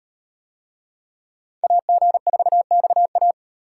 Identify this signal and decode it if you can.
Morse code, heard as a keyed tone, about 35 wpm, 700 Hz. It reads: AG4XA